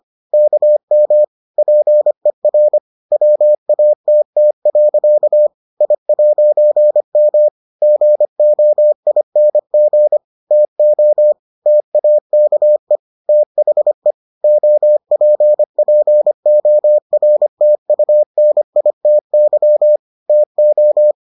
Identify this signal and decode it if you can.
Morse code, heard as a keyed tone, about 25 wpm, 600 Hz. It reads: KM PER WATT. I'M GOING TO TAKE THE OPPORTUNITY TO